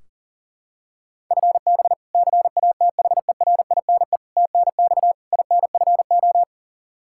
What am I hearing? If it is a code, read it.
Morse code, heard as a keyed tone, about 37 wpm, 700 Hz. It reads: FB CATHERINE TNX INFO
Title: K1USN SST QSO Practice 40wpm